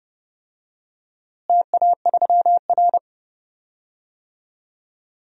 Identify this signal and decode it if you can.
Morse code, heard as a keyed tone, about 30 wpm, 700 Hz. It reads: TA3R
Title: DXCC-100 Call Signs - 4 Characters - Repeated 3x - 30wpm